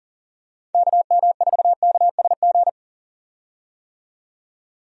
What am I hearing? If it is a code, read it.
Morse code, heard as a keyed tone, about 40 wpm, 700 Hz. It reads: KM4KSG
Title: USA Call Signs - 6 Characters - 40wpm